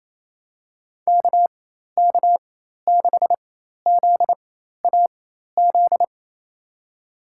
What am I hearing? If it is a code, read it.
Morse code, heard as a keyed tone, about 28 wpm, spaced as for 13 wpm, 700 Hz. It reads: KK6ZAZ
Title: USA Call Signs - 6 Characters - 4x Character Spacing - 28wpm